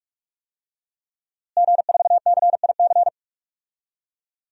Morse code, heard as a keyed tone, about 44 wpm, 700 Hz. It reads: G4CIC
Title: DXCC-100 Call Signs - 5 Characters - Repeated 3x - 45wpm